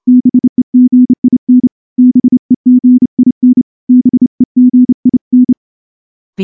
Morse code, heard as a keyed tone, about 29 wpm, 250 Hz. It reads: BEGIN BEGIN BEGIN
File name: morse-training-30WPM